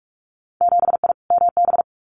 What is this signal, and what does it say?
Morse code, heard as a keyed tone, about 46 wpm, 690 Hz. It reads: 7I MB
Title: Winter Field Day Exchanges Encore 45wpm